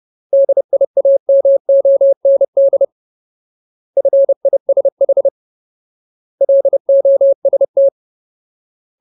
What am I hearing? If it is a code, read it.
Morse code, heard as a keyed tone, about 30 wpm, 550 Hz. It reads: DIAMOND FISH LOST
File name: Ultimate Word List / 30WPM Ultimate